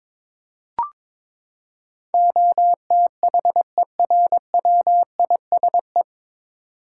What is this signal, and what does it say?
Morse code, heard as a keyed tone, about 21 wpm, 700 Hz. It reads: OTHERWISE
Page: Morse Code Ninja practice 22wpm